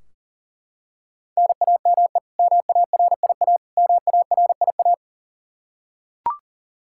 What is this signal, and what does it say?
Morse code, heard as a keyed tone, about 40 wpm, 700 Hz. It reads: NAME MARIA MARIA
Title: Practice QSO - Exchange 1 - Beginning - 40wpm